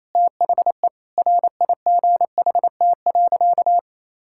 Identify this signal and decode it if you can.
Morse code, heard as a keyed tone, about 28 wpm, 700 Hz. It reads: THE RIGHT.
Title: Sentences from Top 1600 Words 28wpm